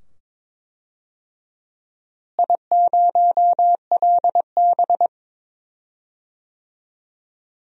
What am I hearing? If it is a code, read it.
Morse code, heard as a keyed tone, about 22 wpm, 700 Hz. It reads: I0LB
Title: DXCC-100 Call Signs - 4 Characters - Repeated 2x - 22wpm